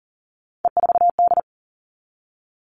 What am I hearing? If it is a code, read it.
Morse code, heard as a keyed tone, about 40 wpm, 700 Hz. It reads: E4D
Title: All Cut Numbers - 3-Digit 40wpm